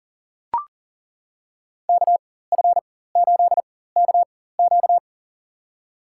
Morse code, heard as a keyed tone, about 40 wpm, spaced as for 19 wpm, 700 Hz. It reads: KF8KQ
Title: USA Call Signs - 4x Character Spacing - 40wpm